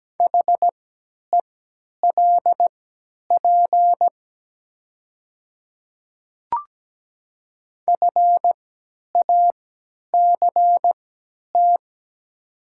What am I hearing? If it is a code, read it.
Morse code, heard as a keyed tone, about 17 wpm, spaced as for 10 wpm, 700 Hz. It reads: HELP FACT
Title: 3x Character Spacing - Top 500 Words - 17wpm